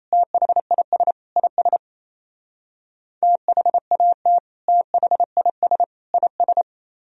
Morse code, heard as a keyed tone, about 34 wpm, 700 Hz. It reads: THIS IS THAT THIS IS
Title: Speed-Racing - Sets of 3 Words 28wpm